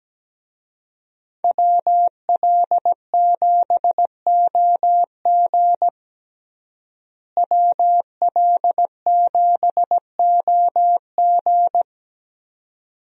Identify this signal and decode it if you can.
Morse code, heard as a keyed tone, about 17 wpm, 700 Hz. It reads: WL7OG WL7OG
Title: USA Call Signs - 5 Characters - Repeated 2x - 17wpm